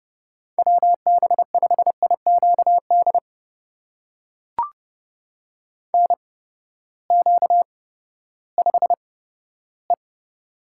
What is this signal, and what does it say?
Morse code, heard as a keyed tone, about 31 wpm, spaced as for 15 wpm, 700 Hz. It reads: WB5IQD NQ5E
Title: USA Call Signs - 8x Character Spacing - 30wpm